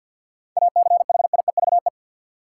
Morse code, heard as a keyed tone, about 51 wpm, 700 Hz. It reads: ACHIEVE
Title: Seven Letter Words - 50wpm